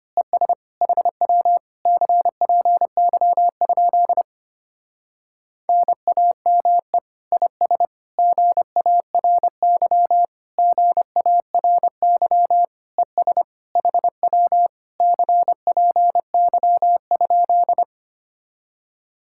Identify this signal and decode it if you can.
Morse code, heard as a keyed tone, about 25 wpm, 700 Hz. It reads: ES HW CPY? NAME IS GARY GARY ES HW CPY?
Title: Speed-Racing Common QSO Phrases - 25wpm